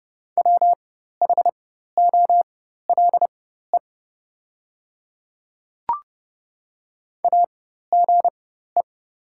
Morse code, heard as a keyed tone, about 30 wpm, spaced as for 14 wpm, 700 Hz. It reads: WHOLE AGE
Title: 4x Character Spacing - Top 500 Words - 30wpm